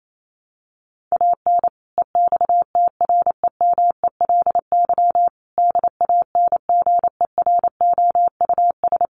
Morse code, heard as a keyed tone, about 28 wpm, 700 Hz. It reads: AN EXTREMELY DANGEROUS
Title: Sentences from Top 1200 Words 28wpm